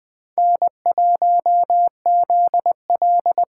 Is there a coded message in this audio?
Yes: Morse code from 0.4 to 3.4 s, heard as a keyed tone, about 21 wpm, 700 Hz. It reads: N1ZL